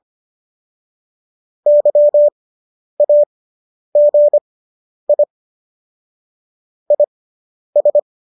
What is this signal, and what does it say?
Morse code, heard as a keyed tone, about 25 wpm, spaced as for 10 wpm, 590 Hz. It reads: YAGI IS